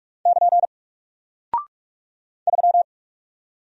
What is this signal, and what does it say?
Morse code, heard as a keyed tone, about 46 wpm, 700 Hz. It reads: <KN> 3
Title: Single Character (Letter:Number:Punctuation:Prosign) Alt 45wpm